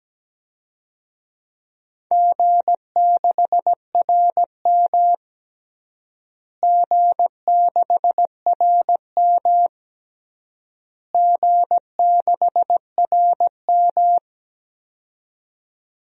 Morse code, heard as a keyed tone, about 17 wpm, 700 Hz. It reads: G6RM G6RM G6RM
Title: DXCC-100 Call Signs - 4 Characters - Repeated 3x - 17wpm